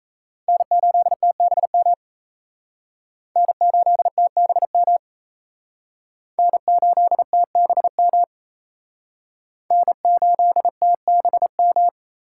Morse code, heard as a keyed tone, about 36 wpm, 700 Hz. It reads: N8TBM N8TBM N8TBM N8TBM